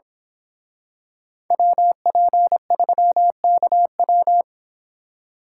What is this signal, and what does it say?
Morse code, heard as a keyed tone, about 26 wpm, 700 Hz. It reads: WP3KW
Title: Speed Racing - USA 2x2 Call Signs 22wpm